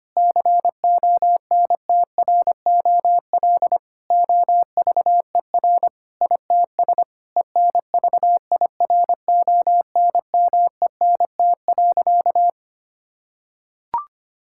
Morse code, heard as a keyed tone, about 25 wpm, 700 Hz. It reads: CONTROL OVER ITS ENVIRONMENT.
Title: Sentences from Top 1000 Words - Review Part 5 - 25wpm